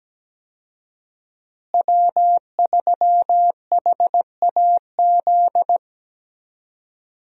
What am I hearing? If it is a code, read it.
Morse code, heard as a keyed tone, about 17 wpm, 700 Hz. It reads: W3HAZ
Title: USA Call Signs - 5 Characters - Repeated 3x - 17wpm